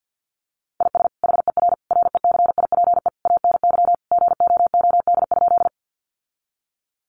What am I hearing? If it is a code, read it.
Morse code, heard as a keyed tone, about 50 wpm, 700 Hz. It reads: IS HER RECIPE ANY GOOD?
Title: Sentences from Top 1200 Words 50wpm